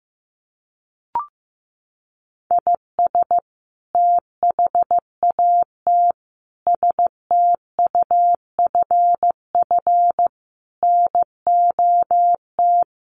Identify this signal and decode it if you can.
Morse code, heard as a keyed tone, about 15 wpm, 700 Hz. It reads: IS THAT STUFF NOT
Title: Sentences from Top 1400 Words 15wpm